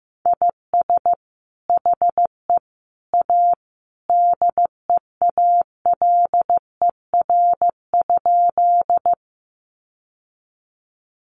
Morse code, heard as a keyed tone, about 15 wpm, 700 Hz. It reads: IS HE A DEALER?
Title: Sentences from Top 1100 Words 15wpm